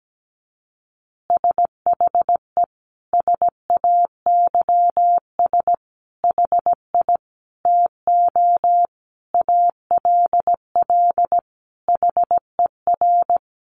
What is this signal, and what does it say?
Morse code, heard as a keyed tone, about 17 wpm, 700 Hz. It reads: SHE SAYS HI TO ALL HER
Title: Easy Phrases 17wpm